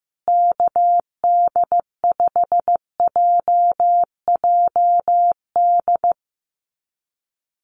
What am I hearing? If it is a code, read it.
Morse code, heard as a keyed tone, about 15 wpm, 700 Hz. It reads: KD5JJD
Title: USA Call Signs Repeated 3x 15wpm